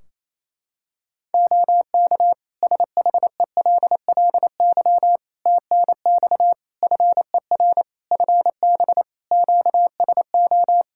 Morse code, heard as a keyed tone, about 27 wpm, 700 Hz. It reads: OK SHELLY TNX FER FB QSO
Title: Practice QSO - Exchange 4 - Ending - 28wpm